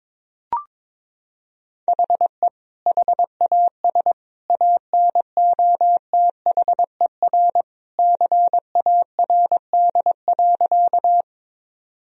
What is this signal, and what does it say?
Morse code, heard as a keyed tone, about 22 wpm, 700 Hz. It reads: HE HAS ANOTHER CARD.
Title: Sentences from Top 1000 Words - Review Part 4 - 22wpm